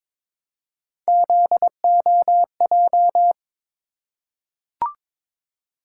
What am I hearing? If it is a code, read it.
Morse code, heard as a keyed tone, about 22 wpm, 700 Hz. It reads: ZOJ
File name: practice / 22wpm / Three Letter-Number Combinations - 22wpm